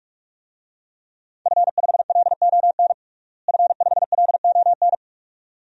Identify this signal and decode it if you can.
Morse code, heard as a keyed tone, about 45 wpm, 700 Hz. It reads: F5LON F5LON
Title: DXCC-100 Call Signs - 5 Characters - Repeated 2x - 45wpm